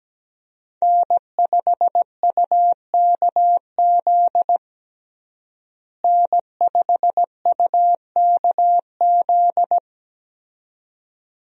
Morse code, heard as a keyed tone, about 17 wpm, 700 Hz. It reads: N5UKZ N5UKZ
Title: USA Call Signs - 5 Characters - Repeated 2x - 17wpm